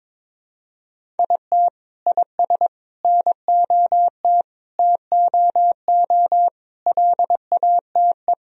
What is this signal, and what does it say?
Morse code, heard as a keyed tone, about 22 wpm, 700 Hz. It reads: IT IS NOT TOO LATE
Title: Sentences from Top 1000 Words - Review Part 1 - 22wpm